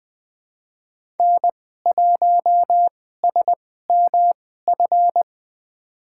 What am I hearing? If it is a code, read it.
Morse code, heard as a keyed tone, about 20 wpm, spaced as for 14 wpm, 700 Hz. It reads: N1SMF
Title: USA Call Signs - 5 Characters - 2x Character Spacing - 20wpm